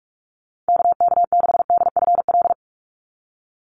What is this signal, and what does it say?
Morse code, heard as a keyed tone, about 44 wpm, 700 Hz. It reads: KK6DFL
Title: USA Call Signs 45wpm